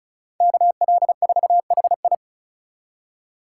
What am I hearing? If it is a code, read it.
Morse code, heard as a keyed tone, about 34 wpm, 700 Hz. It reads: KL4HI
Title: USA Call Signs - 5 Characters - 4x Character Spacing - 35wpm